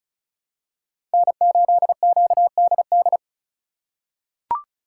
Morse code, heard as a keyed tone, about 36 wpm, 700 Hz. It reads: N8QDD
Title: USA Call Signs - 5 Characters - 2x Character Spacing - 35wpm